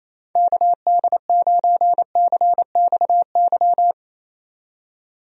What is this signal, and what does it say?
Morse code, heard as a keyed tone, about 27 wpm, 700 Hz. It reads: KD9CXY
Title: USA Call Signs - 6 Characters - 28wpm